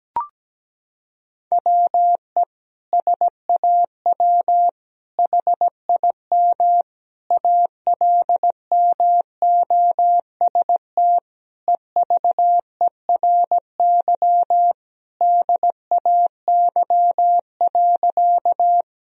Morse code, heard as a keyed tone, about 17 wpm, 700 Hz. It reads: WE SAW HIM ALMOST EVERY DAY.
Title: Sentences from Top 1000 Words - Review Part 4 - 17wpm